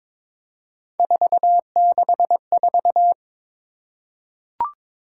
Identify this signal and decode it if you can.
Morse code, heard as a keyed tone, about 23 wpm, 700 Hz. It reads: 464
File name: practice / 22wpm / Three Letter-Number Combinations - 22wpm